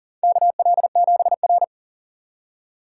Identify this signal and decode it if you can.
Morse code, heard as a keyed tone, about 40 wpm, 700 Hz. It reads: KL7R
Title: USA Call Signs - 4 Characters - 2x Character Spacing - 40wpm